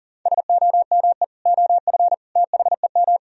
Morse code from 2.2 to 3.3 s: THEM

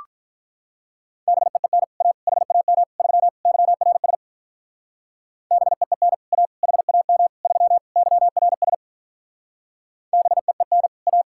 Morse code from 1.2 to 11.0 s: BEEN A HAM 3 YRS BEEN A HAM 3 YRS BEEN

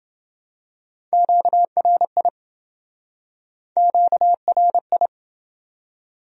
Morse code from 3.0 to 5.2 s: QRI

